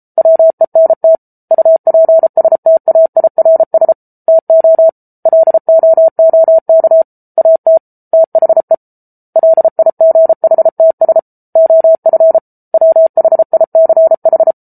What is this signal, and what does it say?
Morse code, heard as a keyed tone, about 34 wpm, 650 Hz. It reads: WENT UPSTAIRS TO LOOK AT THE LIGHTS OF WHICH